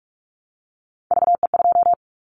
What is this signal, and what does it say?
Morse code, heard as a keyed tone, about 46 wpm, 700 Hz. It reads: VE2
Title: Callsign Prefixes for Top 50 DXCC - 45wpm